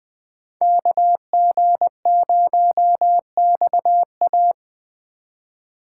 Morse code, heard as a keyed tone, about 21 wpm, 700 Hz. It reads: KG0XA
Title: USA Call Signs - 5 Characters - 2x Character Spacing - 20wpm